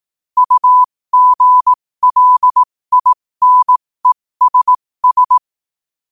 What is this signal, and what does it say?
Morse code, heard as a keyed tone, about 18 wpm, spaced as for 15 wpm, 1000 Hz. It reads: UGLINESS
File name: qotdfiles / 15wpm